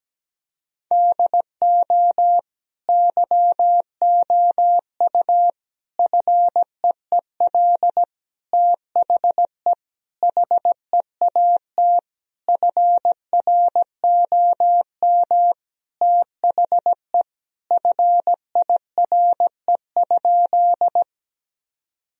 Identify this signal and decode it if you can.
Morse code, heard as a keyed tone, about 17 wpm, 700 Hz. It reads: DO YOU FEEL THE HEAT FROM THE FIRE?
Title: Sentences from Top 1000 Words - Review Part 3 - 17wpm